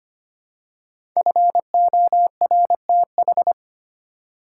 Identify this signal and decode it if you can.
Morse code, heard as a keyed tone, about 25 wpm, 700 Hz. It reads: FORTH
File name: 5 Letter Words - First 3 Letters Given - 25wpm